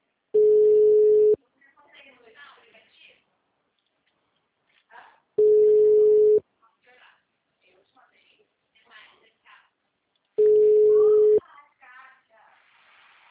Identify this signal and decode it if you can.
Ringback tone (Europe, 425 Hz), with speech.